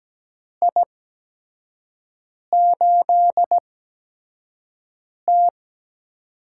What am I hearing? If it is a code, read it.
Morse code, heard as a keyed tone, about 17 wpm, spaced as for 5 wpm, 700 Hz. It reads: I8T
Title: USA Call Signs - 4 Characters - 8x Character Spacing - 17wpm